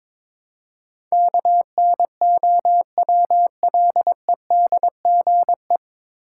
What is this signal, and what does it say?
Morse code, heard as a keyed tone, about 22 wpm, 700 Hz. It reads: KNOWLEDGE
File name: 9 Letter Words - First 3 Letters Given - 22wpm